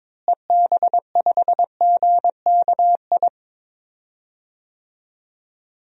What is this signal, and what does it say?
Morse code, heard as a keyed tone, about 22 wpm, 700 Hz. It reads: EB5GKI
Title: DXCC-100 Call Signs - 6 Characters - Repeated 2x - 22wpm